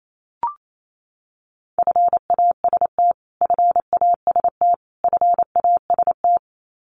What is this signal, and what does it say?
Morse code, heard as a keyed tone, about 28 wpm, 700 Hz. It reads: FAST FAST FAST
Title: Top 500 Words Repeated 28wpm